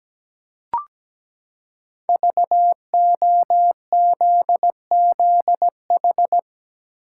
Morse code, heard as a keyed tone, about 17 wpm, 700 Hz. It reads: VOZZH
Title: Code Groups - 5 Letter:Numbers - 17wpm